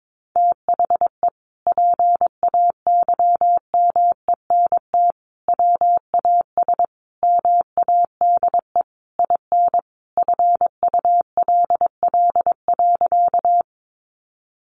Morse code, heard as a keyed tone, about 22 wpm, 700 Hz. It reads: THE PAYMENT WAS MADE IN FULL.
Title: Sentences from Top 1200 Words 22wpm